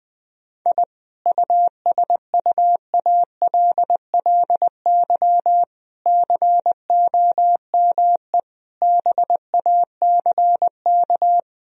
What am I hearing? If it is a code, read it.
Morse code, heard as a keyed tone, about 20 wpm, 700 Hz. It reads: I USUALLY COME BACK
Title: Sentences from Top 1000 Words - Review Part 3 - 20wpm